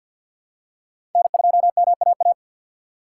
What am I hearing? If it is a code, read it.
Morse code, heard as a keyed tone, about 48 wpm, 700 Hz. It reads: N2MAA
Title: USA Call Signs - 5 Characters - 6x Character Spacing - 50wpm